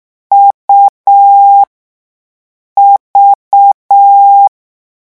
Morse code, heard as a keyed tone, about 6 wpm, spaced as for 5 wpm, 800 Hz. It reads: UV
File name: morse mp3 / lecon3-vitesse 6